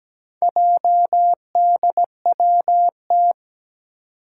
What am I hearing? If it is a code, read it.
Morse code, heard as a keyed tone, about 17 wpm, 700 Hz. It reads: JDWT
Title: Four Challenging Letter-Number Combinations 17wpm